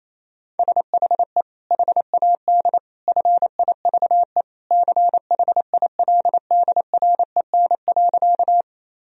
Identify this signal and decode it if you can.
Morse code, heard as a keyed tone, about 28 wpm, 700 Hz. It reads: SHE HAD FIVE CHILDREN.
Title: Sentences from Top 1000 Words - Review Part 2 - 28wpm